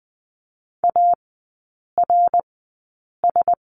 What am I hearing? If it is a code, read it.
Morse code, heard as a keyed tone, about 20 wpm, spaced as for 8 wpm, 700 Hz. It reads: ARS
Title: 053 Lesson 10e - Letter r - Three Characters 20wpm